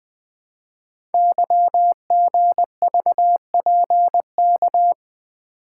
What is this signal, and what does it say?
Morse code, heard as a keyed tone, about 20 wpm, 700 Hz. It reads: YGVPK
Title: Code Groups - 5 Letters - 20wpm